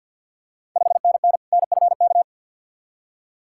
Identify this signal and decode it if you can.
Morse code, heard as a keyed tone, about 50 wpm, 700 Hz. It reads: 5NN NFK
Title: Ontario QSO Party Exchange 50wpm